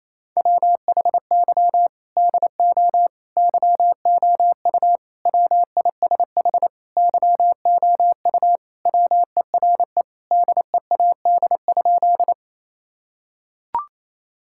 Morse code, heard as a keyed tone, about 27 wpm, 700 Hz. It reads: WHY DO YOU WISH YOU WERE DEAD?
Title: Sentences from Top 1600 Words 28wpm